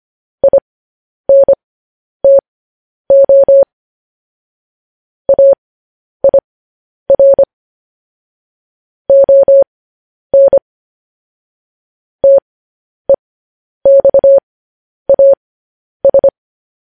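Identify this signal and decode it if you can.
Morse code, heard as a keyed tone, about 25 wpm, spaced as for 10 wpm, 550 Hz. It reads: INTO AIR ON TEXAS
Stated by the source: Morse code news headlines sent at 10.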